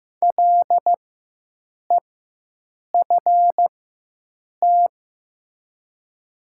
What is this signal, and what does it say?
Morse code, heard as a keyed tone, about 15 wpm, spaced as for 7 wpm, 700 Hz. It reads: LEFT
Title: 4x Character Spacing - Top 500 Words - 15wpm